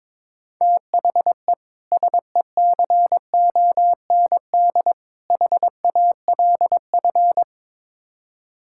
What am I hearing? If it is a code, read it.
Morse code, heard as a keyed tone, about 22 wpm, 700 Hz. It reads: THE SECOND HALF
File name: practice / 22wpm / Build A Phrase - 5-Word Phrases - 22wpm